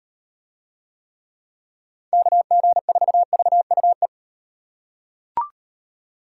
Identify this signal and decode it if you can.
Morse code, heard as a keyed tone, about 38 wpm, 700 Hz. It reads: KG4VUE